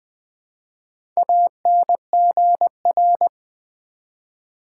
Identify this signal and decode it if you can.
Morse code, heard as a keyed tone, about 20 wpm, 700 Hz. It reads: ANGR
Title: Build A Word - 1 Letter At A Time - 5-Letter Words - 20wpm